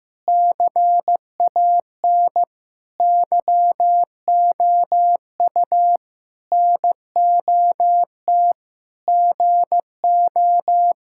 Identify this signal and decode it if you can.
Morse code, heard as a keyed tone, about 15 wpm, 700 Hz. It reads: CAN YOU NOT GO
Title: Sentences from Top 1600 Words 15wpm